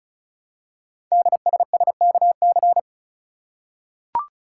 Morse code, heard as a keyed tone, about 36 wpm, 700 Hz. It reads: DSSKC
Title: Code Groups - 5 Letters - 35wpm